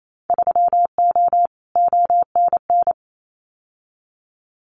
Morse code, heard as a keyed tone, about 28 wpm, 700 Hz. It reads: 3O ONN
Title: Winter Field Day Exchanges 28wpm